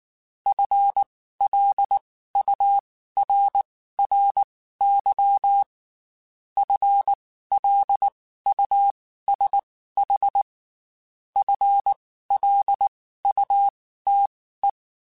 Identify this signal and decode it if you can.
Morse code, heard as a keyed tone, about 19 wpm, spaced as for 14 wpm, 800 Hz. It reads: FLURRY FLUSH FLUTE
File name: Spaced / FL Runs sp mp3s / FL Runs sp @18